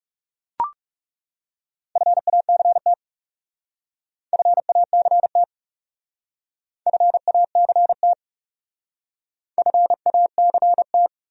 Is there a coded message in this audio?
Yes: Morse code from 2.0 to 11.1 s, heard as a keyed tone, about 40 wpm, 700 Hz. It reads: FACT FACT FACT FACT